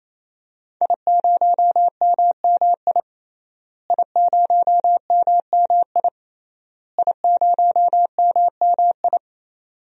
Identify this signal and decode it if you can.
Morse code, heard as a keyed tone, about 28 wpm, 700 Hz. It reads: I0MMI I0MMI I0MMI
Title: DXCC-100 Call Signs - 5 Characters - Repeated 3x - 28wpm